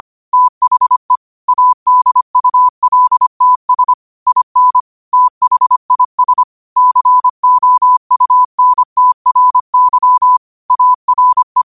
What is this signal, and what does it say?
Morse code, heard as a keyed tone, about 25 wpm, 1000 Hz. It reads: THE ADULTS IN THIS COUNTRY ARE